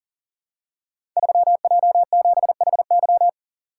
Morse code, heard as a keyed tone, about 38 wpm, 700 Hz. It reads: 3J7HY
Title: Code Groups - 5 Letter:Numbers - 40wpm